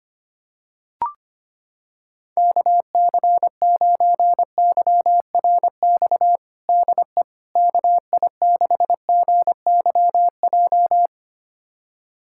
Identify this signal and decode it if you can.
Morse code, heard as a keyed tone, about 25 wpm, 700 Hz. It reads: KC9YRX DE KI6GYJ
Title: USA Callsign DE Callsign 25wpm